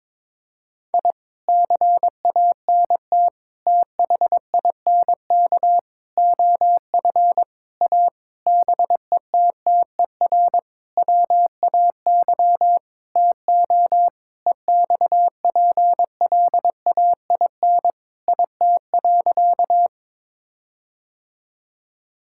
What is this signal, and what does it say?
Morse code, heard as a keyed tone, about 22 wpm, 700 Hz. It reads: I CANT THINK OF A BETTER WAY TO EXPLAIN IT.
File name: practice / 22wpm / Sentences with 8-Word Phrases - 22wpm